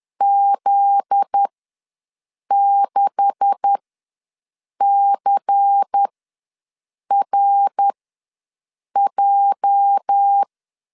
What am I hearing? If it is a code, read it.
Morse code, heard as a keyed tone, about 10 wpm, spaced as for 6 wpm, 800 Hz. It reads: Z6CRJ
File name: LEZIONI / 9 286 ws 10wpm 4032